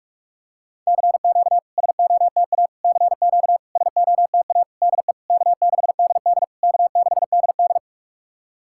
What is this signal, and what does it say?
Morse code, heard as a keyed tone, about 46 wpm, 690 Hz. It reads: CQ SOTA CQ SOTA DE K6DD K6DD
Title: SOTA QSO Practice 45wpm